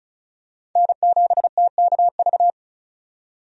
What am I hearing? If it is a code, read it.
Morse code, heard as a keyed tone, about 36 wpm, 700 Hz. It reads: N7TKV
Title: USA Call Signs - 5 Characters - 35wpm